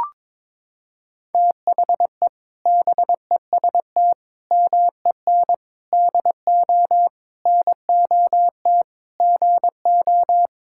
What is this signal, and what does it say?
Morse code, heard as a keyed tone, about 22 wpm, 700 Hz. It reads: THE BEST MEN DO NOT GO
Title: Sentences from Top 1000 Words - Review Part 6 - 22wpm